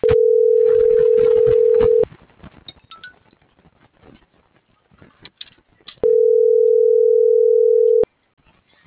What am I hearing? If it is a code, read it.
Ringback tone (North America), 440 + 480 Hz.